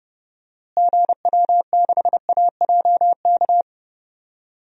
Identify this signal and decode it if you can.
Morse code, heard as a keyed tone, about 31 wpm, 700 Hz. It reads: GW6AJK
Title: DXCC-100 Call Signs - 6 Characters - Repeated 2x - 30wpm